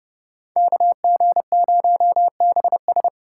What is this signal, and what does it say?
Morse code, heard as a keyed tone, about 28 wpm, 700 Hz. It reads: KG0BS